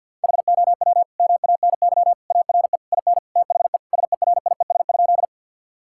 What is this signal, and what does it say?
Morse code, heard as a keyed tone, about 52 wpm, 700 Hz. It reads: HOW MANY ARE IN THE SERIES?
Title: Sentences from Top 1000 Words - Review Part 5 - 50wpm